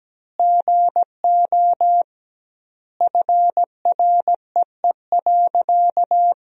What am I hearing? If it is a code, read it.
Morse code, heard as a keyed tone, about 17 wpm, 700 Hz. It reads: GO FREE.
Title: 2x Word-Spacing - Sentences from Top 500 Words - 17wpm